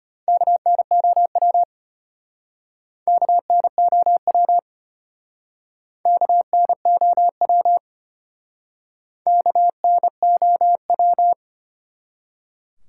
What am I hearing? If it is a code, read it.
Morse code, heard as a keyed tone, about 32 wpm, 700 Hz. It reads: KNOW KNOW KNOW KNOW